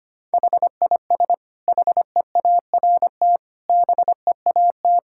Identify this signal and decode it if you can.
Morse code, heard as a keyed tone, about 25 wpm, 700 Hz. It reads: HIS HEART BEAT
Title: Sentences from Top 1600 Words 25wpm